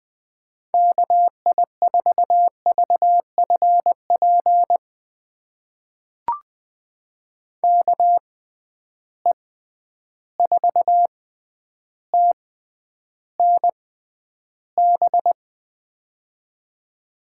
Morse code, heard as a keyed tone, about 20 wpm, spaced as for 10 wpm, 700 Hz. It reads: KI4VFP KE4TNB
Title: USA Call Signs - 6 Characters - 6x Character Spacing - 20wpm